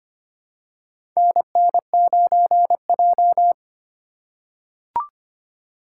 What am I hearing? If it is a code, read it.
Morse code, heard as a keyed tone, about 25 wpm, 700 Hz. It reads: NN9J
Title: USA Call Signs - 4 Characters - 4x Character Spacing - 25wpm